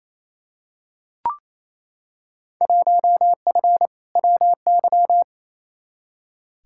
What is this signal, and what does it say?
Morse code, heard as a keyed tone, about 28 wpm, 700 Hz. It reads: 1F WY
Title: Field Day Exchanges Encore - 28wpm